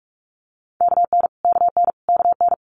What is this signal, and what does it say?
Morse code, heard as a keyed tone, about 46 wpm, 700 Hz. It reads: KN KN KN
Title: Multi-letter Phonemes Repeated 3x 45wpm